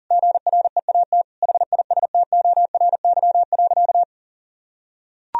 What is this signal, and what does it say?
Morse code, heard as a keyed tone, about 40 wpm, 700 Hz. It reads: GREAT HISTORY.
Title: Sentences from Top 1000 Words - Review Part 4 - 40wpm